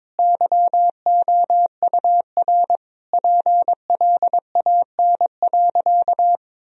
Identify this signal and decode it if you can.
Morse code, heard as a keyed tone, about 22 wpm, 700 Hz. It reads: YOUR PLAN.
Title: Sentences from Top 1000 Words - Review Part 2 - 22wpm